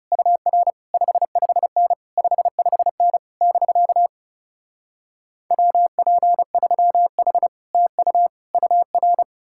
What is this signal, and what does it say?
Morse code, heard as a keyed tone, about 40 wpm, 700 Hz. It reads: UR 55N 55N <BK> WP3H TU UR